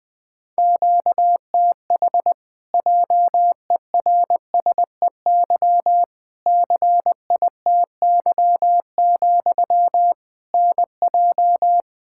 Morse code, heard as a keyed tone, about 20 wpm, 700 Hz. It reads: QTH JERSEY CITY, NJ
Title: Practice QSO - Exchange 1 - Beginning - 20wpm